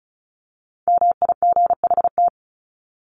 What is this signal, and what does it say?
Morse code, heard as a keyed tone, about 34 wpm, 700 Hz. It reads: MIGHT
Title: Top 300 Words 35wpm